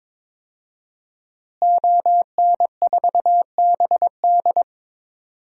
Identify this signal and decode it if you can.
Morse code, heard as a keyed tone, about 21 wpm, 700 Hz. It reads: ON4BD
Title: DXCC-100 Call Signs - 5 Characters - Repeated 2x - 22wpm